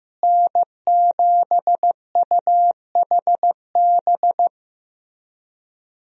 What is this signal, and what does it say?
Morse code, heard as a keyed tone, about 15 wpm, 700 Hz. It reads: N7UHB
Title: USA Call Signs - 5 Characters - Repeated 3x - 15wpm